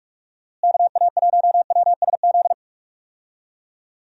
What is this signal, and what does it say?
Morse code, heard as a keyed tone, about 44 wpm, 700 Hz. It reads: KA1WSZ
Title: USA Call Signs - 6 Characters - Repeated 2x - 45wpm